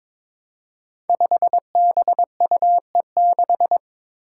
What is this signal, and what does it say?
Morse code, heard as a keyed tone, about 22 wpm, 700 Hz. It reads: 5BUE6
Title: Five Challenging Letter-Number Combinations 22wpm